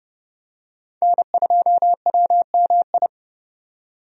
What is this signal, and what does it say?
Morse code, heard as a keyed tone, about 30 wpm, 700 Hz. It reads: N2WMI
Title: USA Call Signs - 5 Characters - 30wpm